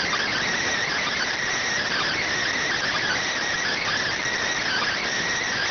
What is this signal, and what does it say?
Emergency Alert System (SAME) tones, heard as harsh buzzing bursts. It is heard from 0.9 to 5.0 s.
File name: findings / CT CORONAL 710mm 10mm scout.gen-35 rx